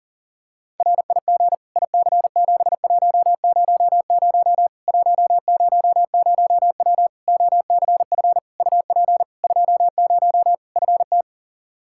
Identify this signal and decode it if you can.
Morse code, heard as a keyed tone, about 40 wpm, 700 Hz. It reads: RIG IC7100 100W OCF UP 20 FT
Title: Practice QSO - Exchange 2 - Info - 40wpm